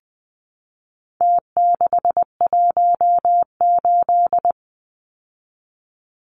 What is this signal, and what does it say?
Morse code, heard as a keyed tone, about 20 wpm, 700 Hz. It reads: T618
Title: Numbers - Contest Style - 4-Digit 20wpm